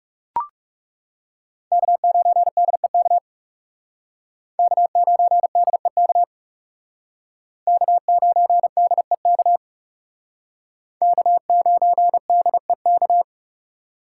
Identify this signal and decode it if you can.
Morse code, heard as a keyed tone, about 36 wpm, 700 Hz. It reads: K9DEK K9DEK K9DEK K9DEK
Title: Speed Racing - USA 1x3 Call Signs 30wpm